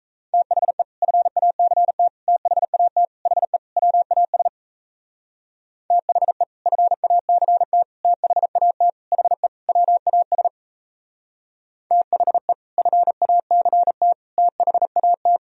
Morse code, heard as a keyed tone, about 35 wpm, 700 Hz. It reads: THE FACT THAT HE WAS THE FACT THAT HE WAS THE FACT THAT
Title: Speed-Racing - Sets of 5 Words 28wpm